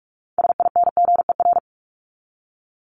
Morse code, heard as a keyed tone, about 44 wpm, 700 Hz. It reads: SINGER
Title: Top 1501-1750 Words 45wpm